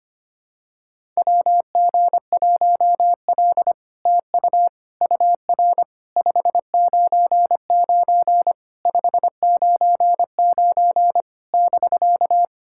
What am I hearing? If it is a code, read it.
Morse code, heard as a keyed tone, about 25 wpm, 700 Hz. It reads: WG1L TU UR 599 599 <BK>
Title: SOTA QSO Practice 25wpm